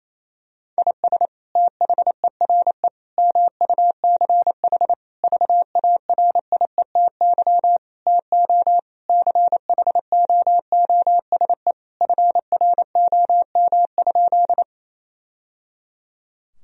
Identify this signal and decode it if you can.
Morse code, heard as a keyed tone, about 28 wpm, 700 Hz. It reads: IS THERE MUCH VARIETY TO CHOOSE FROM?
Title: Sentences from Top 1000 Words - Review Part 4 - 28wpm